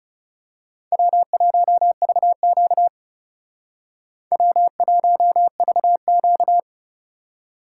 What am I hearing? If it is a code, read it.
Morse code, heard as a keyed tone, about 34 wpm, 700 Hz. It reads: W1VQ W1VQ
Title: Speed Racing - USA Call Signs 30wpm